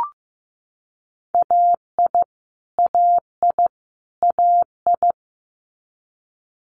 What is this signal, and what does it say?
Morse code, heard as a keyed tone, about 15 wpm, 700 Hz. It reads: AI AI AI
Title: Multi-letter Phonemes Repeated 3x 15wpm